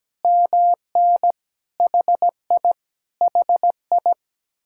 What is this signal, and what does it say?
Morse code, heard as a keyed tone, about 17 wpm, 700 Hz. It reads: MN HI HI
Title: Speed-Racing Common QSO Phrases - 17wpm